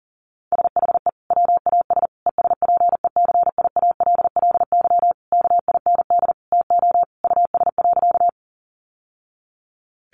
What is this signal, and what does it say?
Morse code, heard as a keyed tone, about 40 wpm, 700 Hz. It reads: SHE WAS ESPECIALLY KIND TO US.
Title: Sentences from Top 700 Words 40wpm